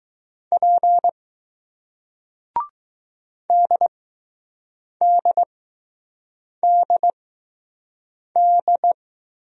Morse code, heard as a keyed tone, about 20 wpm, spaced as for 6 wpm, 700 Hz. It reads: P DDDD